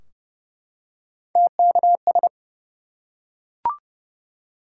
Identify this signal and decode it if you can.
Morse code, heard as a keyed tone, about 30 wpm, 700 Hz. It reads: TKS
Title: Top 90 QSO Elements 30wpm